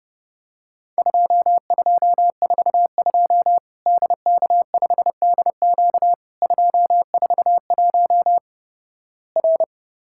Morse code, heard as a keyed tone, about 32 wpm, 700 Hz. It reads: 2242 DK5DQ 241 R
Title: Worked All Europe DX Contest - QTC Exchange - 30wpm